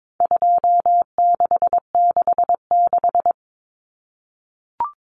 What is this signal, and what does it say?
Morse code, heard as a keyed tone, about 22 wpm, 700 Hz. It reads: 2666
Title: Common Cut Numbers - 4-Digit 22wpm